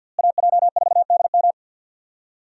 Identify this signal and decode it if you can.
Morse code, heard as a keyed tone, about 52 wpm, 700 Hz. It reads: AJ4DM
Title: USA Call Signs - 5 Characters - Repeated 2x - 50wpm